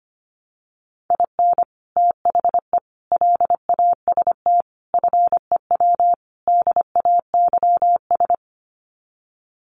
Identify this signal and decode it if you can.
Morse code, heard as a keyed tone, about 25 wpm, 700 Hz. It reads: IN THE LAST FEW DAYS
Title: Sets of 5 Words Encore2 25wpm